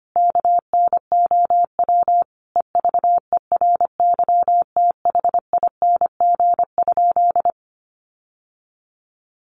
Morse code, heard as a keyed tone, about 25 wpm, 700 Hz. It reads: KNOW EVERYTHING?
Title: Sentences from Top 700 Words 25wpm